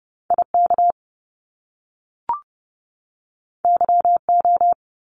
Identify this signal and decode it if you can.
Morse code, heard as a keyed tone, about 30 wpm, 700 Hz. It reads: IK YO
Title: Top Two Letter Combinations in Words 30wpm